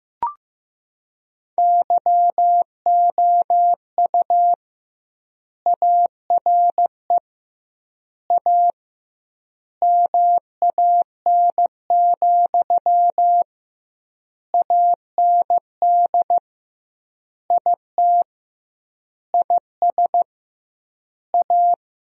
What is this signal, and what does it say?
Morse code, heard as a keyed tone, about 15 wpm, 700 Hz. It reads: YOU ARE A MAN, AND IT IS A
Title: 2x Word-Spacing - Sentences from Top 200 Words - 15wpm